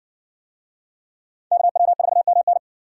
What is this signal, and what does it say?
Morse code, heard as a keyed tone, about 54 wpm, 700 Hz. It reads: KW4MN